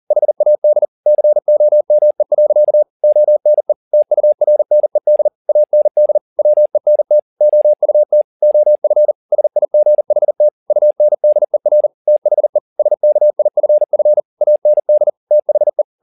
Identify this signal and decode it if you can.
Morse code, heard as a keyed tone, about 40 wpm, 600 Hz. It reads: HAD COME. ONE TURNED AND WENT OUT OF SIGHT UNDER THE SKIFF AND THE
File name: the old man and the sea 2 40wpm